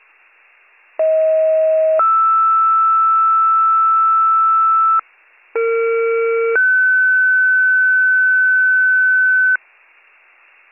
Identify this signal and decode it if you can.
Two-tone page, heard as a one-second tone, then a three-second tone, about 640 then 1320 Hz, about 460 then 1500 Hz. It is heard from 1.0 to 9.6 s.